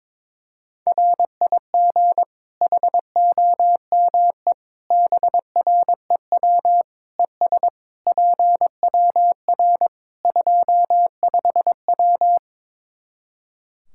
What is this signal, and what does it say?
Morse code, heard as a keyed tone, about 22 wpm, 700 Hz. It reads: RIG HOME BREW ES PWR 25W
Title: Practice QSO - Exchange 2 - Info - 22wpm